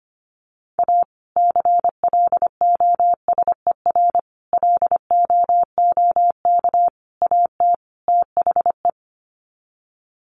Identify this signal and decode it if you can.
Morse code, heard as a keyed tone, about 25 wpm, 700 Hz. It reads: A CLOSER LOOK AT THE
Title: Sets of 5 Words Encore 25wpm